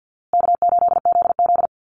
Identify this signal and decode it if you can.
Morse code, heard as a keyed tone, about 50 wpm, 700 Hz. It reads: K8ZZ
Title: Call Signs Repeated 3x - 50wpm